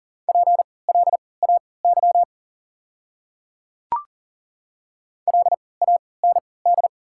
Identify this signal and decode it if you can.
Morse code, heard as a keyed tone, about 40 wpm, spaced as for 23 wpm, 690 Hz. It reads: PLAY LAND
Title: 3x Character Spacing - Top 300 Words - 40wpm